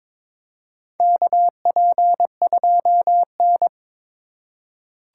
Morse code, heard as a keyed tone, about 22 wpm, 700 Hz. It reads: KP2N
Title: K1USN SST QSO Practice - 22wpm